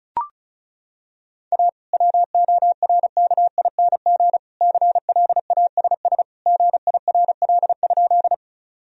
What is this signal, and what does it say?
Morse code, heard as a keyed tone, about 36 wpm, 700 Hz. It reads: A WORKING CLASS GIRL?
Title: Sentences from Top 1000 Words - Review Part 4 - 35wpm